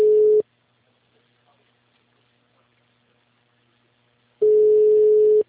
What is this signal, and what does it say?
Ringback tone (Europe, 425 Hz).
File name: out- -2041- - - .89811